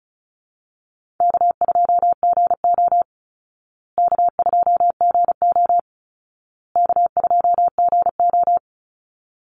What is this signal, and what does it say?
Morse code, heard as a keyed tone, about 34 wpm, 700 Hz. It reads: K2GO K2GO K2GO
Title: Easy Call Signs Repeated 3x - 35wpm